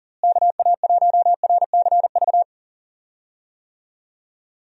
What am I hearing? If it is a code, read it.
Morse code, heard as a keyed tone, about 39 wpm, 700 Hz. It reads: KA1RCV
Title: USA Call Signs - 6 Characters - Repeated 2x - 40wpm